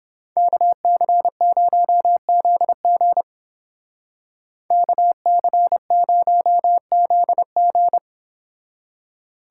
Morse code, heard as a keyed tone, about 28 wpm, 700 Hz. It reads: KC0ZG KC0ZG